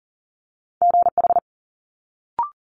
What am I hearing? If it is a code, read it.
Morse code, heard as a keyed tone, about 42 wpm, 700 Hz. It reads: GH
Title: Multi-letter Phonemes Quick Fire 40wpm